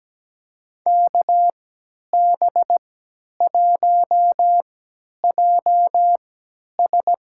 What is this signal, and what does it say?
Morse code, heard as a keyed tone, about 17 wpm, spaced as for 10 wpm, 700 Hz. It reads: KB1JS
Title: USA Call Signs - 5 Characters - 3x Character Spacing - 17wpm